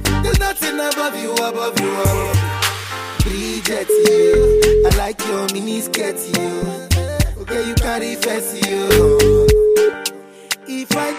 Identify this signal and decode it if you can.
Ringback tone (Europe, 425 Hz), with noise.